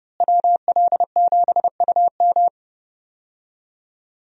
Morse code, heard as a keyed tone, about 32 wpm, 700 Hz. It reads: WL7UM
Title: USA Call Signs - 5 Characters - Repeated 3x - 30wpm